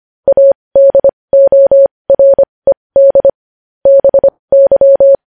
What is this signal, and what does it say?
Morse code, heard as a keyed tone, about 25 wpm, spaced as for 20 wpm, 550 Hz. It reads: ADORED BY
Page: news-headlines-in-morse-code-at-20-wpm